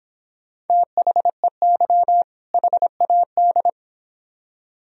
Morse code, heard as a keyed tone, about 26 wpm, 700 Hz. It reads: THEY HAD